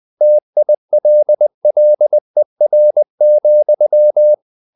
Morse code, heard as a keyed tone, about 20 wpm, 600 Hz. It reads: TILLER,
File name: the old man and the sea part 2 20wpm